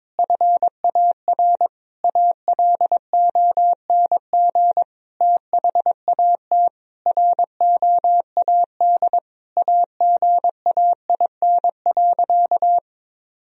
Morse code, heard as a keyed tone, about 22 wpm, 700 Hz. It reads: FAR ALONG THAT ROAD AGAIN.
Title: Sentences from Top 1000 Words - Review Part 2 - 22wpm